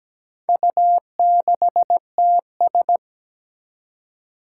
Morse code, heard as a keyed tone, about 17 wpm, 700 Hz. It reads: U6TS
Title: Four Challenging Letter-Number Combinations 17wpm